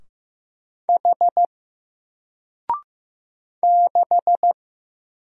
Morse code, heard as a keyed tone, about 15 wpm, 700 Hz. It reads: H 6
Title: Single Letter-Number Alt 15wpm